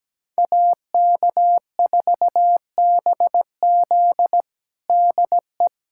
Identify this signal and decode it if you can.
Morse code, heard as a keyed tone, about 17 wpm, 700 Hz. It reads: AK4BZ DE
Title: USA Callsign DE Callsign - 5 Characters - 17wpm